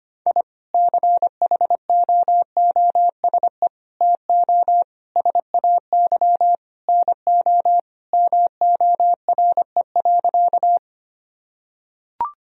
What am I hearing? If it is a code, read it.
Morse code, heard as a keyed tone, about 25 wpm, 700 Hz. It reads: I CHOOSE TO SAY NO MORE.
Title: Sentences from Top 1000 Words - Review Part 4 - 25wpm